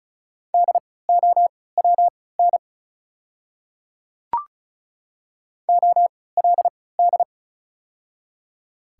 Morse code, heard as a keyed tone, about 33 wpm, spaced as for 19 wpm, 690 Hz. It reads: DOWN OLD
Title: 3x Character Spacing - Top 300 Words - 35wpm